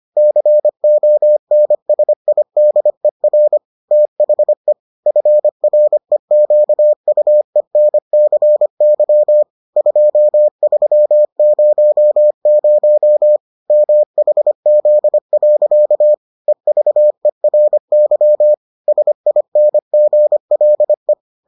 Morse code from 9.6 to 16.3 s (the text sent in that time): 2300 MHZ.